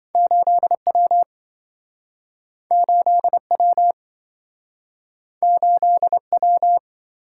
Morse code, heard as a keyed tone, about 28 wpm, 700 Hz. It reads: 8W 8W 8W